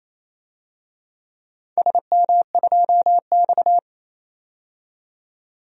Morse code, heard as a keyed tone, about 27 wpm, 690 Hz. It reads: SM2X
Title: DXCC-100 Call Signs - 4 Characters - 28wpm